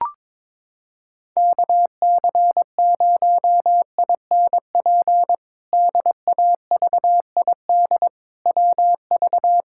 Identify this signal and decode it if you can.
Morse code, heard as a keyed tone, about 22 wpm, 700 Hz. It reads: KC0INP DAVID WV
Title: K1USN SST QSO Practice - 22wpm